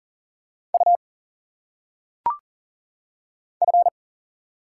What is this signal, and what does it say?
Morse code, heard as a keyed tone, about 40 wpm, 700 Hz. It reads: U F
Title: Single Letters - Phonetic Pronunciation 40wpm